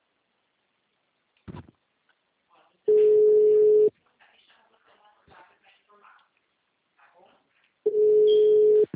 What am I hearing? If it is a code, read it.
Ringback tone (Europe, 425 Hz), with speech.